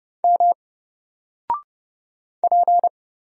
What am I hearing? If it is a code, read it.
Morse code, heard as a keyed tone, about 30 wpm, 700 Hz. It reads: M P
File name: Single-Letter-Number-30wpm